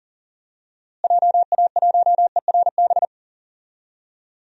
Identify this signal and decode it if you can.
Morse code, heard as a keyed tone, about 40 wpm, 700 Hz. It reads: JA1ERB
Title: DXCC-100 Call Signs - 6 Characters - Repeated 3x - 40wpm